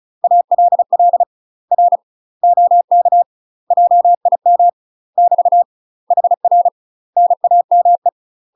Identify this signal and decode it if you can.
Morse code, heard as a keyed tone, about 34 wpm, 700 Hz. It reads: ALL R OK JIM <BT> HR NAME
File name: UK MASTER QSOs TRAINING / 35WPM / UK MASTER QSOs 0040